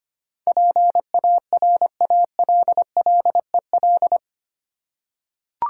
Morse code, heard as a keyed tone, about 25 wpm, 700 Hz. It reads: PARALLEL
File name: 8 Letter Words - First 3 Letters Given - 25wpm